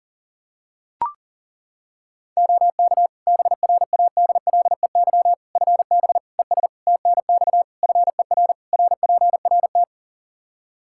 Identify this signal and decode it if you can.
Morse code, heard as a keyed tone, about 40 wpm, 700 Hz. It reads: OK BRADLEY FB ES TNX FER RPRT
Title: Practice QSO - Exchange 2 - Info - 40wpm